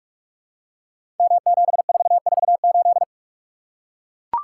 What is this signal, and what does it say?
Morse code, heard as a keyed tone, about 44 wpm, 700 Hz. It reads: M7448
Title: Code Groups - 5 Letter:Numbers - 45wpm